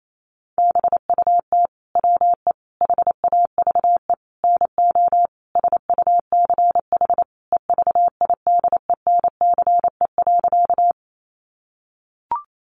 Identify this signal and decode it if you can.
Morse code, heard as a keyed tone, about 28 wpm, 700 Hz. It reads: BUT WE HAVE NO SUCH EVIDENCE.
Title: Sentences from Top 1300 Words 28wpm